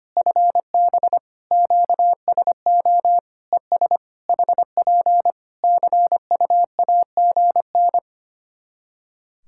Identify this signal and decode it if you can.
Morse code, heard as a keyed tone, about 25 wpm, 700 Hz. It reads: FB QSO ES HP CUAGN
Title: K1USN SST QSO Practice 25wpm